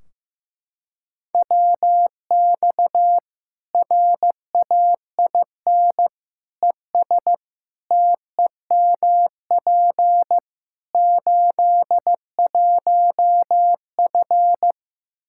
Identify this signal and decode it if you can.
Morse code, heard as a keyed tone, about 15 wpm, 700 Hz. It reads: WX RAIN ES TEMP 81F
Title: Practice QSO - Exchange 2 - Info - 15wpm